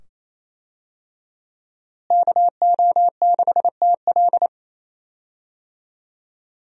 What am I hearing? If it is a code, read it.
Morse code, heard as a keyed tone, about 27 wpm, 700 Hz. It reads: KO6TL
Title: USA Call Signs - 5 Characters - Repeated 2x - 28wpm